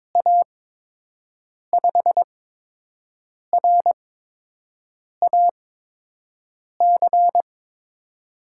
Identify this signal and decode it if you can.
Morse code, heard as a keyed tone, about 22 wpm, spaced as for 6 wpm, 700 Hz. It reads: A5RAC
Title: USA Call Signs - 6 Characters - 8x Character Spacing - 22wpm